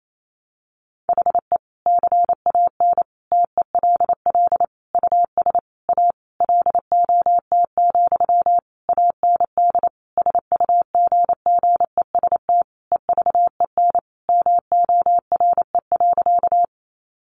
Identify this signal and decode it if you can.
Morse code, heard as a keyed tone, about 28 wpm, 700 Hz. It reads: HE CAN TELL US A LOT, AND SUGGEST EVEN MORE.
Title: Sentences from Top 1300 Words 28wpm